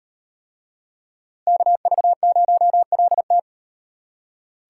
Morse code, heard as a keyed tone, about 38 wpm, 700 Hz. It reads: KV0LT